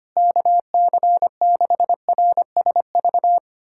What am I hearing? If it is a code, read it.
Morse code, heard as a keyed tone, about 25 wpm, 700 Hz. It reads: KC6RSV
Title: USA Callsign DE Callsign 25wpm